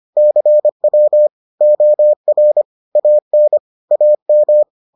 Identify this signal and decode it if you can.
Morse code, heard as a keyed tone, about 25 wpm, 600 Hz. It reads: CW OR AN AM